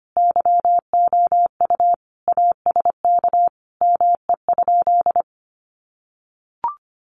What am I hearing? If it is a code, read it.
Morse code, heard as a keyed tone, about 25 wpm, 700 Hz. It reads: YOU ASK ME?
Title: Sentences from Top 200 Words 25wpm